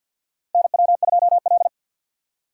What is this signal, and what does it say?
Morse code, heard as a keyed tone, about 53 wpm, 700 Hz. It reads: NW2L
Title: USA Call Signs - 4 Characters - 6x Character Spacing - 50wpm